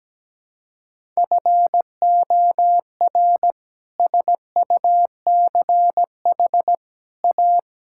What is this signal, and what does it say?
Morse code, heard as a keyed tone, about 17 wpm, 700 Hz. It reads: FOR SUCH A
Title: Sentences from Top 1000 Words - Review Part 6 - 17wpm